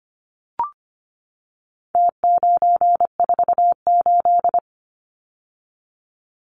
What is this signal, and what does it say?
Morse code, heard as a keyed tone, about 25 wpm, 700 Hz. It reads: T948
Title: Numbers - Contest Style - 4-Digit 25wpm